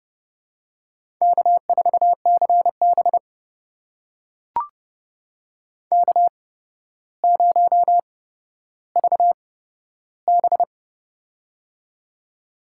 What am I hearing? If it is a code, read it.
Morse code, heard as a keyed tone, about 32 wpm, spaced as for 13 wpm, 700 Hz. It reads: K4CB K0VB
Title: USA Call Signs - 4 Characters - 8x Character Spacing - 30wpm